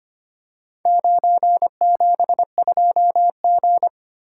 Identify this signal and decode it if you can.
Morse code, heard as a keyed tone, about 26 wpm, 700 Hz. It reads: 972G
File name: practice / 25wpm / Four Letter-Number Combinations - 25wpm